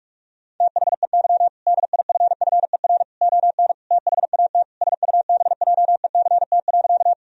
Morse code, heard as a keyed tone, about 41 wpm, 700 Hz. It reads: THEY DIFFER ON THAT SUBJECT.
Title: Sentences from Top 1000 Words - Review Part 1 - 45wpm